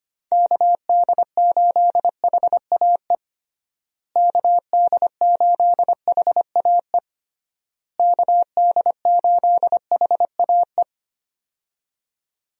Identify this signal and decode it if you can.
Morse code, heard as a keyed tone, about 25 wpm, 700 Hz. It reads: KD8HAE KD8HAE KD8HAE
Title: USA Call Signs - 6 Characters - Repeated 3x - 25wpm